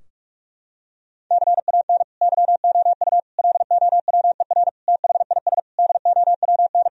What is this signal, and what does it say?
Morse code, heard as a keyed tone, about 41 wpm, 700 Hz. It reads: CAN YOU LOWER THIS DOWN
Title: Sentences from Top 1000 Words - Review Part 4 - 45wpm